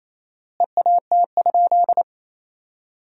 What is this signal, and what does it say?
Morse code, heard as a keyed tone, about 28 wpm, 700 Hz. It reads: EAT?
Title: 4x Word-Spacing - Sentences from Top 300 Words - 28wpm